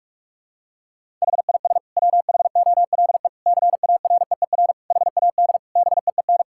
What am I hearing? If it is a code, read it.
Morse code, heard as a keyed tone, about 47 wpm, 700 Hz. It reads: HIS WHOLE CAREER HAD BEEN
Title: Sentences from Top 1000 Words - Review Part 6 - 45wpm